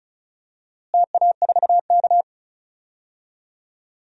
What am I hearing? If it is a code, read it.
Morse code, heard as a keyed tone, about 34 wpm, 700 Hz. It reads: TA4K
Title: DXCC-100 Call Signs - 4 Characters - Repeated 2x - 35wpm